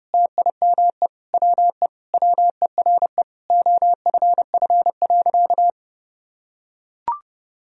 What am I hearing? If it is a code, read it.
Morse code, heard as a keyed tone, about 30 wpm, 700 Hz. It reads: TIME WE WERE OFF.
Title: Sentences from Top 1000 Words - Review Part 1 - 30wpm